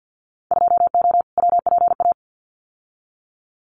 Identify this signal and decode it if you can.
Morse code, heard as a keyed tone, about 50 wpm, 700 Hz. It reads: 2O WPA
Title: Winter Field Day Exchanges Encore 50wpm